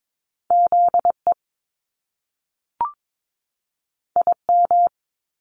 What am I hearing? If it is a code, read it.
Morse code, heard as a keyed tone, about 22 wpm, 700 Hz. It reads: ZE IM
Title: Top Two Letter Combinations in Words 22wpm